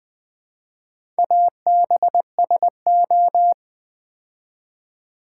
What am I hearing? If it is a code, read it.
Morse code, heard as a keyed tone, about 20 wpm, 700 Hz. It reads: ABSO
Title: Build A Word - 2 Letters At A Time - 6-Letter Words - 20wpm